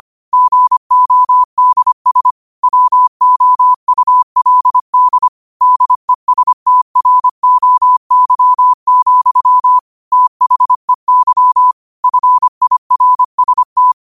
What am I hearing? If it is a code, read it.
Morse code, heard as a keyed tone, about 25 wpm, 1000 Hz. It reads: GODS WOULD DESTROY, THEY FIRST